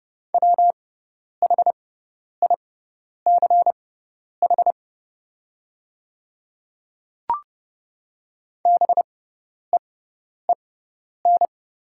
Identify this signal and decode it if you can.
Morse code, heard as a keyed tone, about 30 wpm, spaced as for 10 wpm, 690 Hz. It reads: WHICH BEEN
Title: 6x Character Spacing - Top 100 Words - 30wpm